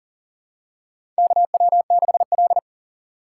Morse code, heard as a keyed tone, about 40 wpm, 700 Hz. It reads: KW6L
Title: USA Call Signs - 3x Character Spacing - 40wpm